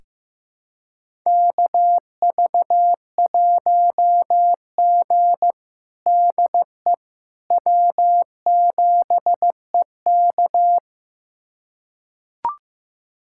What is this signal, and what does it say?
Morse code, heard as a keyed tone, about 15 wpm, 700 Hz. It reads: KV1G DE W7EK
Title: USA Callsign DE Callsign - 4 Characters - 15wpm